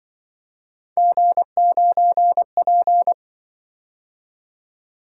Morse code, heard as a keyed tone, about 25 wpm, 700 Hz. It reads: G9P